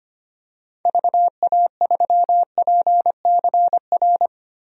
Morse code, heard as a keyed tone, about 25 wpm, 700 Hz. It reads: VA3PCR
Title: DXCC-100 Call Signs - 6 Characters - Repeated 3x - 25wpm